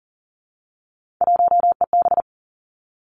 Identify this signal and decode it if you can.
Morse code, heard as a keyed tone, about 42 wpm, 700 Hz. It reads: 1EB
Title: Three Challenging Letter-Number Combinations 40wpm